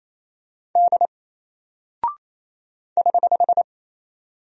Morse code, heard as a keyed tone, about 28 wpm, 700 Hz. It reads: D <HH>
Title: Single Character (Letter:Number:Punctuation:Prosign) 28wpm